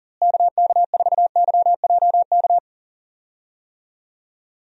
Morse code, heard as a keyed tone, about 39 wpm, 700 Hz. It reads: KK4YJK
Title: USA Call Signs - 6 Characters - Repeated 2x - 40wpm